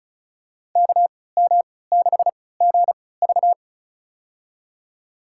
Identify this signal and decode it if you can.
Morse code, heard as a keyed tone, about 34 wpm, spaced as for 20 wpm, 700 Hz. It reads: KM6GV
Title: USA Call Signs - 5 Characters - 3x Character Spacing - 35wpm